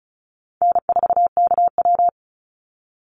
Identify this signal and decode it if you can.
Morse code, heard as a keyed tone, about 36 wpm, 700 Hz. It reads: N4KW
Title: Call Signs - 35wpm